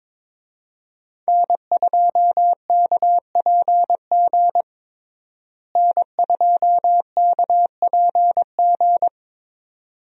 Morse code, heard as a keyed tone, about 22 wpm, 700 Hz. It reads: N2KPG N2KPG
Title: USA Call Signs - 5 Characters - Repeated 2x - 22wpm